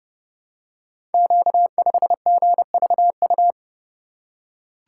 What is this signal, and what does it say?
Morse code, heard as a keyed tone, about 28 wpm, 690 Hz. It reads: Q5GVU
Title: Code Groups - 5 Letter:Numbers - 30wpm